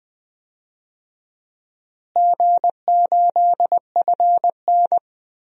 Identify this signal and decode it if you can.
Morse code, heard as a keyed tone, about 21 wpm, 700 Hz. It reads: G8FN
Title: DXCC-100 Call Signs - 4 Characters - Repeated 3x - 20wpm